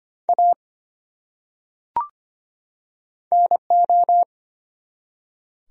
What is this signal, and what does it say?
Morse code, heard as a keyed tone, about 25 wpm, 700 Hz. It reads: A NO
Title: Top 100 Words 25wpm